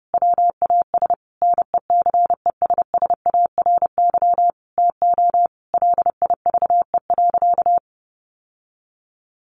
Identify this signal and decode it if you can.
Morse code, heard as a keyed tone, about 28 wpm, 700 Hz. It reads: WAS NECESSARY TO LIVE.
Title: Sentences from Top 800 Words 30wpm